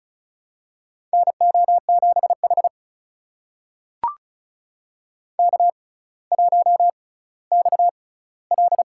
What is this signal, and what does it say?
Morse code, heard as a keyed tone, about 34 wpm, spaced as for 18 wpm, 700 Hz. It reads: NO7H K1XL
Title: USA Call Signs - 4 Characters - 6x Character Spacing - 35wpm